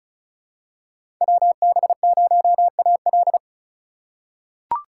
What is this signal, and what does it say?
Morse code, heard as a keyed tone, about 36 wpm, 700 Hz. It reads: WB0AL
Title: USA Call Signs - 3x Character Spacing - 35wpm